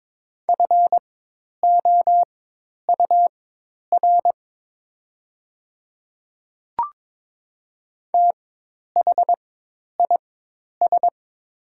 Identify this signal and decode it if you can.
Morse code, heard as a keyed tone, about 22 wpm, spaced as for 10 wpm, 700 Hz. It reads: FOUR THIS
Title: 4x Character Spacing - Top 300 Words - 22wpm